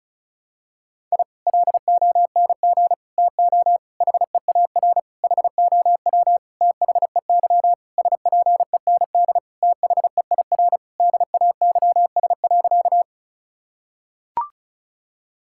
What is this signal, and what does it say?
Morse code, heard as a keyed tone, about 36 wpm, 700 Hz. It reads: I LONG TO HEAR HOW THEY SPEND THEIR DAYS.
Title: Sentences from Top 1000 Words - Review Part 5 - 35wpm